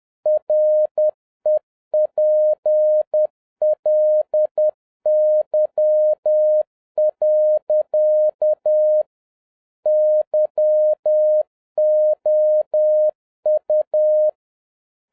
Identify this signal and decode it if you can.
Morse code, heard as a keyed tone, about 10 wpm, 610 Hz. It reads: REPLY. YOU